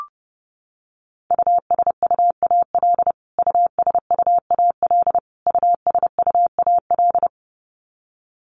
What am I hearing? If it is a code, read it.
Morse code, heard as a keyed tone, about 30 wpm, 700 Hz. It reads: USUAL USUAL USUAL
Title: Top 500 Words Repeated 30wpm